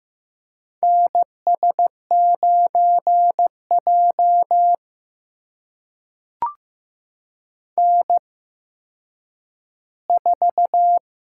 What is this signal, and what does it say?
Morse code, heard as a keyed tone, about 15 wpm, spaced as for 9 wpm, 700 Hz. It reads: NS9J N4
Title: USA Call Signs - 4 Characters - 8x Character Spacing - 15wpm